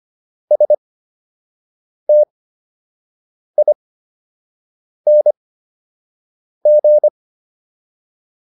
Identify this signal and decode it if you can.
Morse code, heard as a keyed tone, about 25 wpm, spaced as for 6 wpm, 600 Hz. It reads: STING